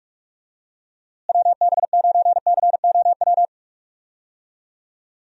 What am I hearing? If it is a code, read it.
Morse code, heard as a keyed tone, about 42 wpm, 700 Hz. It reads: WB9COW
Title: USA Call Signs - 6 Characters - Repeated 2x - 45wpm